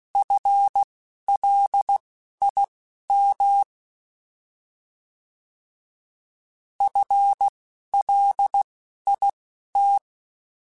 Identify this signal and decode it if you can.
Morse code, heard as a keyed tone, about 16 wpm, spaced as for 11 wpm, 800 Hz. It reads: FLIM FLIT
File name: Alphabetical / Alphabetical mp3s / Fs @10